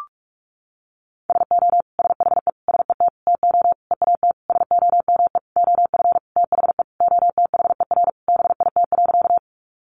Morse code, heard as a keyed tone, about 44 wpm, 700 Hz. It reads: SO SHE SET TO EAT SOME OF THE OTHER BIT.
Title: Sentences from Top 900 Words 45wpm